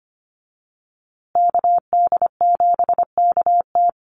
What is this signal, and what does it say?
Morse code, heard as a keyed tone, about 25 wpm, 700 Hz. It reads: KD7KT
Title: USA Call Signs Repeated 2x 25wpm